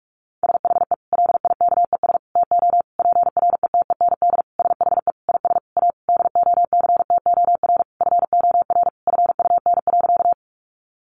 Morse code, heard as a keyed tone, about 44 wpm, 700 Hz. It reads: SHE LIKES TO PRETEND SHE IS A DOCTOR FOR FUN.
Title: Additional Easy Phrases 45wpm